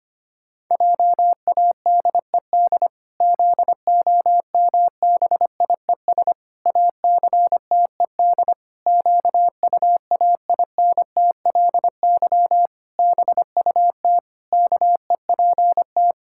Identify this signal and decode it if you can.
Morse code, heard as a keyed tone, about 25 wpm, 700 Hz. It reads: JADED ZOMBIES ACTED QUAINTLY BUT KEPT
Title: Pangram Challenge v2 - 25wpm